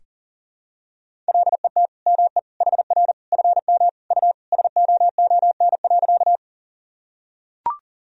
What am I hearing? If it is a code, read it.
Morse code, heard as a keyed tone, about 37 wpm, 700 Hz. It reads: LET ME HR FM U SOON.
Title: Sentences from Top 400 Words and Abbreviations 40wpm